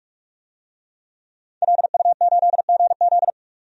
Morse code, heard as a keyed tone, about 46 wpm, 700 Hz. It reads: LU8GZ
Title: DXCC-100 Call Signs - 5 Characters - 45wpm